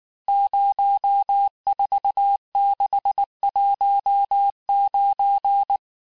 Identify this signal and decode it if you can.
Morse code, heard as a keyed tone, about 19 wpm, 800 Hz. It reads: 04619
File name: cours-du-lundi files / cours-1595-teste-CHIFFRE-vitesse18-MPM